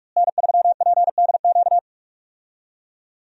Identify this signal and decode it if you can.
Morse code, heard as a keyed tone, about 46 wpm, 700 Hz. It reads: N3PDQ
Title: USA Call Signs - 5 Characters - Repeated 2x - 45wpm